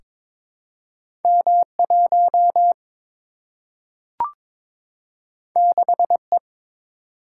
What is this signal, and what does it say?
Morse code, heard as a keyed tone, about 22 wpm, 700 Hz. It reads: M 1 6 E
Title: Single-Letter Single-Number 22wpm